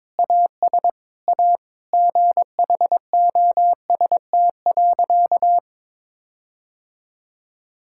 Morse code, heard as a keyed tone, about 21 wpm, 700 Hz. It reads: AS A GHOST.
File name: practice / 22wpm / Sentences with 8-Word Phrases - 22wpm